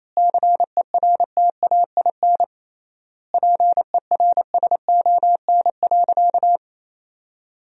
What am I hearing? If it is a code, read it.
Morse code, heard as a keyed tone, about 28 wpm, 700 Hz. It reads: CERTAIN PERSON.
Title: 3x Word-Spacing - Sentences from Top 500 Words - 28wpm